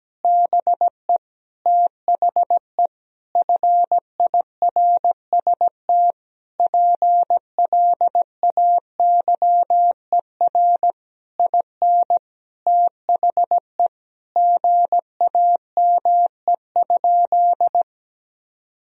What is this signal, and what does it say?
Morse code, heard as a keyed tone, about 17 wpm, 700 Hz. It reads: BE THE FIRST PLAYER IN THE GAME?
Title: Sentences from Top 1000 Words - Review Part 4 - 17wpm